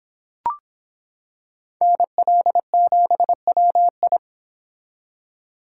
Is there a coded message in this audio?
Yes: Morse code from 1.8 to 4.2 s, heard as a keyed tone, about 25 wpm, 700 Hz. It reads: NL7WI